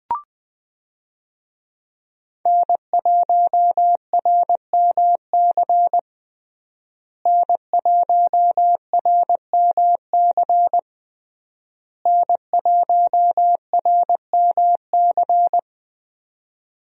Morse code, heard as a keyed tone, about 20 wpm, 700 Hz. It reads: N1RMC N1RMC N1RMC
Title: USA Call Signs - 5 Characters - Repeated 3x - 20wpm